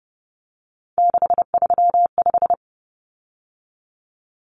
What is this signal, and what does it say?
Morse code, heard as a keyed tone, about 30 wpm, 700 Hz. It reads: 635
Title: Common Cut Numbers - 3-Digit 30wpm